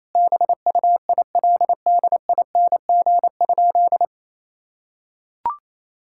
Morse code, heard as a keyed tone, about 28 wpm, 700 Hz. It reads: BUILDING?
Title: Sentences from Top 1000 Words - Review Part 5 - 28wpm